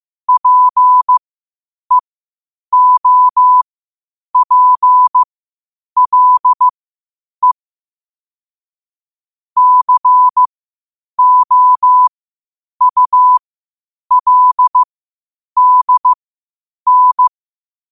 Morse code, heard as a keyed tone, about 15 wpm, spaced as for 8 wpm, 1000 Hz. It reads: PEOPLE COULDN
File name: qotdfiles / 8wpm